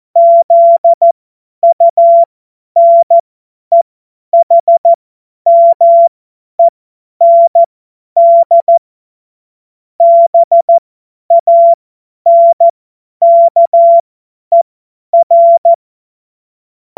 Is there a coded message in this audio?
Yes: Morse code from 0.2 to 15.7 s, heard as a keyed tone, about 14 wpm, spaced as for 10 wpm, 680 Hz. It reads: ZUNEHMEND BANKER